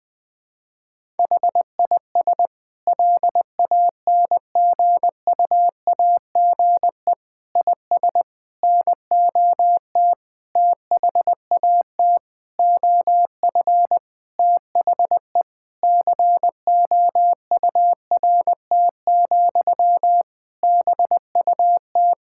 Morse code, heard as a keyed tone, about 20 wpm, 700 Hz. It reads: HIS LANGUAGE IS NOT THAT OF THE COURT, BUT
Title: Sentences from Top 1600 Words 20wpm